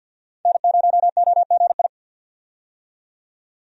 Morse code, heard as a keyed tone, about 51 wpm, 700 Hz. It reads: N0OGI
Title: USA Call Signs - 5 Characters - 50wpm